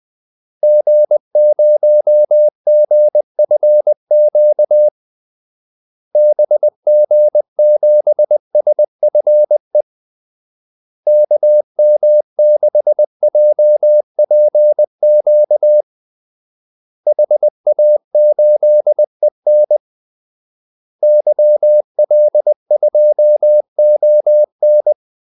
Morse code, heard as a keyed tone, about 20 wpm, 600 Hz. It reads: G0GFQ BG7SFE KM6JPQ HA8EN YL2ON
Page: Kaldesignaler 20 wpm